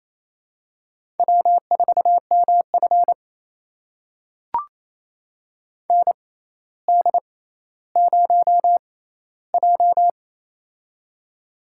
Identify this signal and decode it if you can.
Morse code, heard as a keyed tone, about 28 wpm, spaced as for 14 wpm, 700 Hz. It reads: W4MF ND0J
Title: USA Call Signs - 4 Characters - 6x Character Spacing - 28wpm